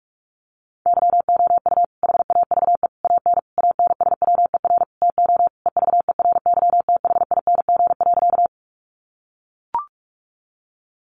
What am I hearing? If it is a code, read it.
Morse code, heard as a keyed tone, about 44 wpm, 700 Hz. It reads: YOU HAVE AN ANSWER TO EVERYTHING.
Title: Sentences from Top 700 Words 45wpm